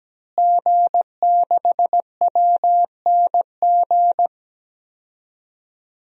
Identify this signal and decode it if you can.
Morse code, heard as a keyed tone, about 17 wpm, 700 Hz. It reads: G6WNG
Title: DXCC-100 Call Signs - 5 Characters - Repeated 3x - 17wpm